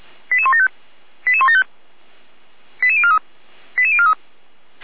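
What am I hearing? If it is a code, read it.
Five-tone selective call (ZVEI), heard as quick runs of five tones, tones of about 70 ms. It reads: 88175 88175 88043 88043
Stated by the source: Five-tone ZVEI selective call received on an SDR.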